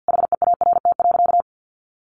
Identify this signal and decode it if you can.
Morse code, heard as a keyed tone, about 50 wpm, 700 Hz. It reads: HEART.
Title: Sentences from Top 1400 Words 50wpm